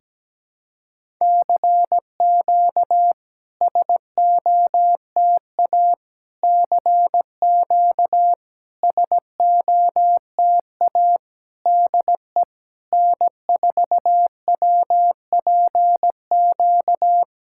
Morse code, heard as a keyed tone, about 17 wpm, 700 Hz. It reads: CQ SOTA CQ SOTA DE N4WPQ
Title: SOTA QSO Practice 17wpm